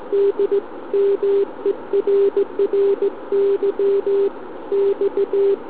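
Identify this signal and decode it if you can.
Morse code, heard as a keyed tone, about 17 wpm, 400 Hz. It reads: D MERRY X
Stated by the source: SM7ALC heard via tropo on 144 MHz with an FT817 and a quarter-wave antenna on a central-heating radiator.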